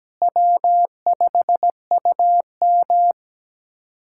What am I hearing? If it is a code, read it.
Morse code, heard as a keyed tone, about 17 wpm, 700 Hz. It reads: W5UM
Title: USA Callsign DE Callsign - 4 Characters - 17wpm